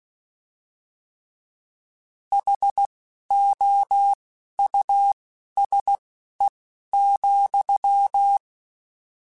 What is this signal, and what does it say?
Morse code, heard as a keyed tone, about 16 wpm, spaced as for 11 wpm, 800 Hz. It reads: HOUSE,
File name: General / General-mp3s / Houses @10